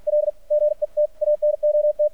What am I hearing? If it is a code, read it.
Morse code, heard as a keyed tone, about 52 wpm, 600 Hz. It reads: L GET ANOT
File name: archived / CRSnet